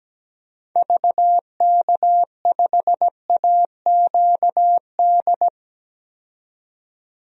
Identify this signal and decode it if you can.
Morse code, heard as a keyed tone, about 17 wpm, 700 Hz. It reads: VK5AQD
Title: DXCC-100 Call Signs - 6 Characters - 17wpm